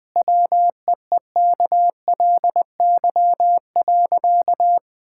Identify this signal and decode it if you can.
Morse code, heard as a keyed tone, about 20 wpm, 700 Hz. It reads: WEEKLY.
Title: Easy Sentences with Hints - 20wpm